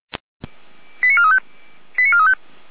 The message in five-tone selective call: 89435 89435

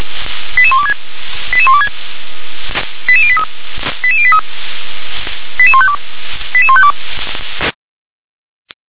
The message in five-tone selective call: 88126 88126 88083 88083 88152 88152

